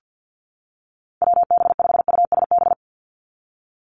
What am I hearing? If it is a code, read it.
Morse code, heard as a keyed tone, about 50 wpm, 700 Hz. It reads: WB5USB
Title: NA SOTA Chaser Callsigns 50wpm